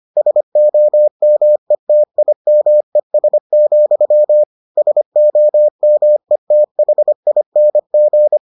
Morse code, heard as a keyed tone, about 25 wpm, 600 Hz. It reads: SOMETIMES, SOMETHING